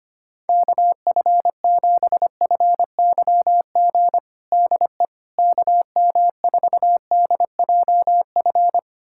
Morse code, heard as a keyed tone, about 25 wpm, 700 Hz. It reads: KF7FYG DE KM4DJF
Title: USA Callsign DE Callsign 25wpm